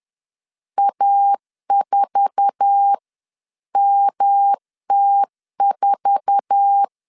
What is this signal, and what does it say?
Morse code, heard as a keyed tone, about 10 wpm, spaced as for 9 wpm, 800 Hz. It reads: A4MT4
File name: LEZIONI / 8 YJ45 / 8 YJ4 10wpm 2633